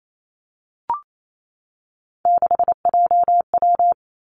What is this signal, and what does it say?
Morse code, heard as a keyed tone, about 28 wpm, 700 Hz. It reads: 6JW
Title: Three Challenging Letter-Number Combinations 28wpm